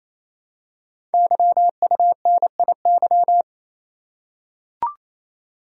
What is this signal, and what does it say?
Morse code, heard as a keyed tone, about 27 wpm, 700 Hz. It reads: YUNIY
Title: Code Groups - 5 Letters - 28wpm